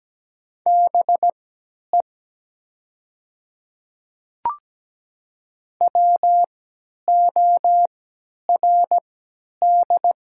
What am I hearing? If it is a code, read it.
Morse code, heard as a keyed tone, about 17 wpm, spaced as for 10 wpm, 700 Hz. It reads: BE WORD
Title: 3x Character Spacing - Top 300 Words - 17wpm